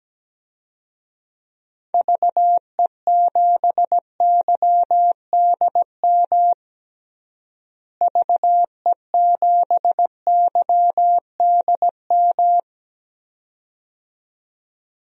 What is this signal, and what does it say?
Morse code, heard as a keyed tone, about 17 wpm, 700 Hz. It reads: VE7YDM VE7YDM
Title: DXCC-100 Call Signs - 6 Characters - Repeated 2x - 17wpm